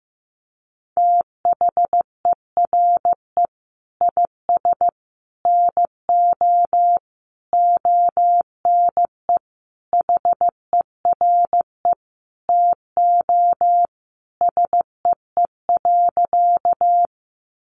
Morse code, heard as a keyed tone, about 15 wpm, 700 Hz. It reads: THERE IS NO ONE HERE TO SEE.
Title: Sentences from Top 200 Words 15wpm